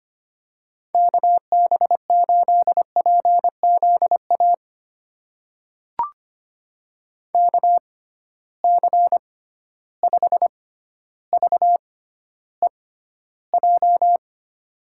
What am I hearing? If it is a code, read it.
Morse code, heard as a keyed tone, about 25 wpm, spaced as for 13 wpm, 700 Hz. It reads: KB8PZA KC5VEJ
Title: USA Call Signs - 6 Characters - 6x Character Spacing - 25wpm